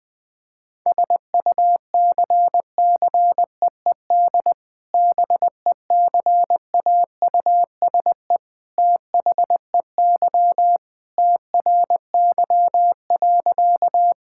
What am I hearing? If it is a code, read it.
Morse code, heard as a keyed tone, about 20 wpm, 700 Hz. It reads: SUCCEED BECAUSE THEY TRY.
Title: Sentences with 2-Word Phrases - 20wpm